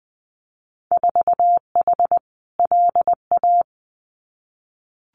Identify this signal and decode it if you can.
Morse code, heard as a keyed tone, about 20 wpm, 700 Hz. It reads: 4H LA
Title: Winter Field Day Exchanges Encore 20wpm